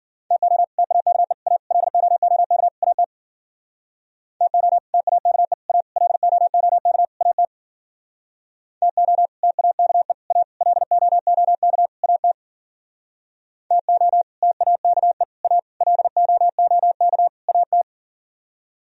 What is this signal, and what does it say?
Morse code, heard as a keyed tone, about 49 wpm, 700 Hz. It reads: TO TAKE A LOOK AT TO TAKE A LOOK AT TO TAKE A LOOK AT TO TAKE A LOOK AT
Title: Speed-Racing - Sets of 5 Words 40wpm